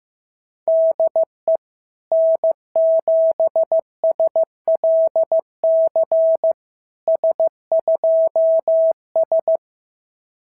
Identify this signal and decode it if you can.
Morse code, heard as a keyed tone, about 15 wpm, 660 Hz. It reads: DE N7SLC S2S
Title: SOTA QSO Practice 15wpm